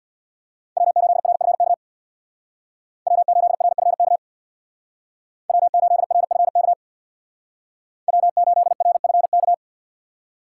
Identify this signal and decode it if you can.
Morse code, heard as a keyed tone, about 64 wpm, 700 Hz. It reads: W8RFK W8RFK W8RFK W8RFK